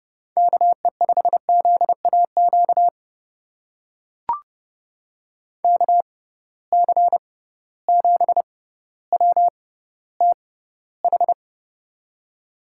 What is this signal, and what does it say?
Morse code, heard as a keyed tone, about 30 wpm, spaced as for 15 wpm, 700 Hz. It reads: KE5ZAQ KC7WTH
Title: USA Call Signs - 6 Characters - 6x Character Spacing - 30wpm